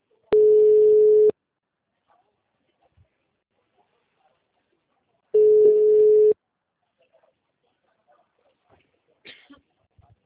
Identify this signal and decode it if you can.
Ringback tone (Europe, 425 Hz).